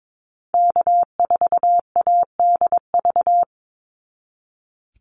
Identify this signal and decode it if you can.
Morse code, heard as a keyed tone, about 22 wpm, 700 Hz. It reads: K4ADV
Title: NA SOTA Activator Callsigns 22wpm